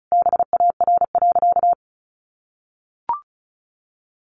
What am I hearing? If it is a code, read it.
Morse code, heard as a keyed tone, about 34 wpm, 700 Hz. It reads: BAR.
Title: Easy Phrases 35wpm